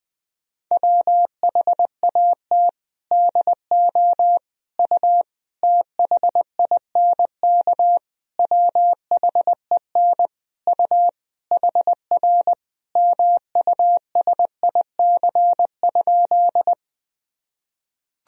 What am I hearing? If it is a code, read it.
Morse code, heard as a keyed tone, about 20 wpm, 700 Hz. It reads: WHAT DO U THINK WHEN U HR MUSIC?
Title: Sentences from Top 300 Words and Abbreviations 20wpm